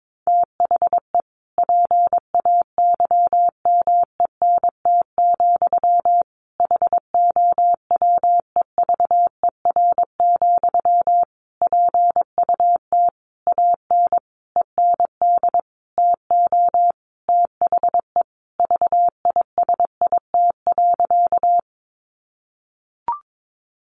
Morse code, heard as a keyed tone, about 22 wpm, 700 Hz. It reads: THE PAYMENT, HOWEVER, PUT AN END TO THE VISIT.
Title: Sentences from Top 1200 Words 22wpm